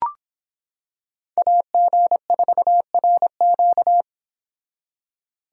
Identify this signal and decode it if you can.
Morse code, heard as a keyed tone, about 26 wpm, 700 Hz. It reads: AG4RQ